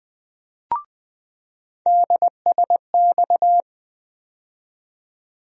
Morse code, heard as a keyed tone, about 20 wpm, 700 Hz. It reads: DSX
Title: USA Call Sign Postfixes - 3 Letters 20wpm